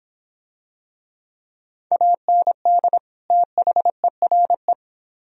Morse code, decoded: AND THERE

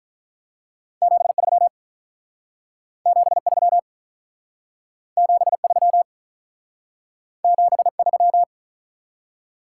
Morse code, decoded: 73 73 73 73

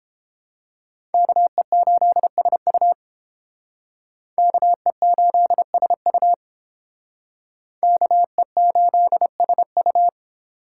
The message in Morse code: KE8SU KE8SU KE8SU

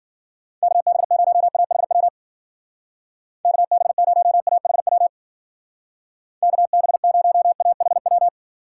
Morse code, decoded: KB0AHW KB0AHW KB0AHW

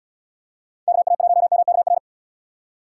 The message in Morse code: KA1MKR

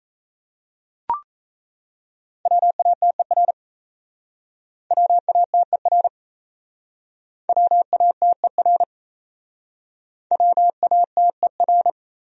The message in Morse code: WATER WATER WATER WATER

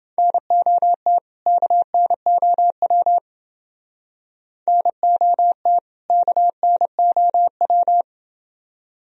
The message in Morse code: NOT KNOW NOT KNOW

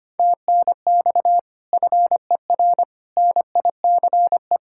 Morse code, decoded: TNX FER NICE

